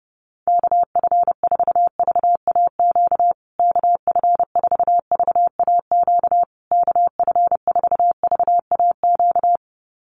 Morse code, heard as a keyed tone, about 30 wpm, 700 Hz. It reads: KF4VAQ KF4VAQ KF4VAQ
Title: USA Call Signs Repeated 3x 30wpm